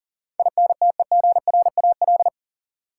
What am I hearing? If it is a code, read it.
Morse code, heard as a keyed tone, about 40 wpm, 700 Hz. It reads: INTEGRAL
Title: Sets of 5 Words - New Word Drill - 40wpm